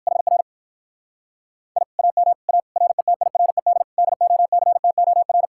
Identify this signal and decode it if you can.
Morse code, heard as a keyed tone, about 56 wpm, 700 Hz. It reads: HR I AM A RETIRED DOCTOR
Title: Speed-Racing Common QSO Phrases - 45wpm